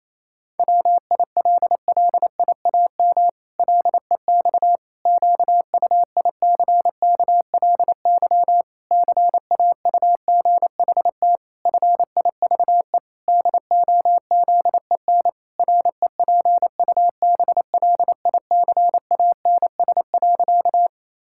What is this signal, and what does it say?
Morse code, heard as a keyed tone, about 28 wpm, 700 Hz. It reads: WILLIAM LEX QUICKLY CAUGHT FIVE DOZEN REPUBLICANS.
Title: Pangram Challenge v2 - 28wpm